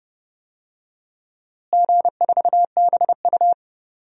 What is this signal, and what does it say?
Morse code, heard as a keyed tone, about 30 wpm, 700 Hz. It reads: G4BU
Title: DXCC-100 Call Signs - 4 Characters - Repeated 3x - 30wpm